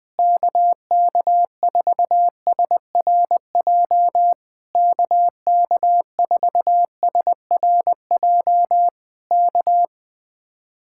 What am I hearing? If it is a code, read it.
Morse code, heard as a keyed tone, about 20 wpm, 700 Hz. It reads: KK4SRJ KK4SRJ K
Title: Speed-Racing Common QSO Phrases Encore - 17wpm